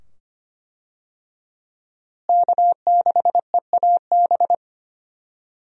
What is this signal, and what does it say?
Morse code, heard as a keyed tone, about 25 wpm, 700 Hz. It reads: K6EAB
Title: USA Call Signs - 5 Characters - Repeated 2x - 25wpm